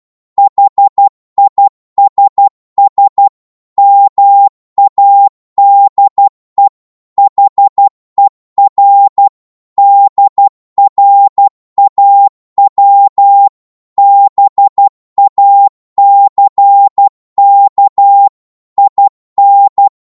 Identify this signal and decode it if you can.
Morse code, heard as a keyed tone, about 12 wpm, 800 Hz. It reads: HISS MADE HER DRAW BACK IN
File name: Morse / psg-12wpm